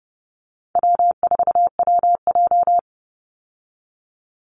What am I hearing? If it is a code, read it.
Morse code, heard as a keyed tone, about 30 wpm, 700 Hz. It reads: W4WJ
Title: Easy Call Signs Repeated 3x - 30wpm